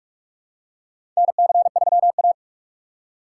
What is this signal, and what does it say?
Morse code, heard as a keyed tone, about 41 wpm, 700 Hz. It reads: NC3A
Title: USA Call Signs - 4 Characters - 4x Character Spacing - 45wpm